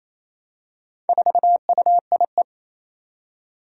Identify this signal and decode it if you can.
Morse code, heard as a keyed tone, about 27 wpm, 700 Hz. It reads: 4UIE
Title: Four Challenging Letter-Number Combinations 28wpm